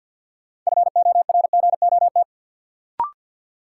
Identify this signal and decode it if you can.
Morse code, heard as a keyed tone, about 47 wpm, 700 Hz. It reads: FORGOT
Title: Sentences from Top 1400 Words - New Word Drill - 50wpm